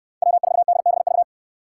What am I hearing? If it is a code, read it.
Morse code, heard as a keyed tone, about 69 wpm, 700 Hz. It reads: W4DBV